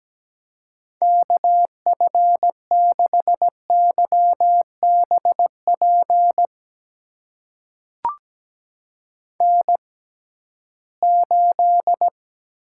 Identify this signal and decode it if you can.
Morse code, heard as a keyed tone, about 17 wpm, spaced as for 13 wpm, 700 Hz. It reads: KF6YBP N8
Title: USA Call Signs - 6x Character Spacing - 17wpm